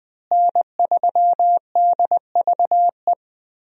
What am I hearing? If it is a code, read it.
Morse code, heard as a keyed tone, about 21 wpm, 700 Hz. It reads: N3DVE
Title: USA Call Signs - 5 Characters - Repeated 3x - 20wpm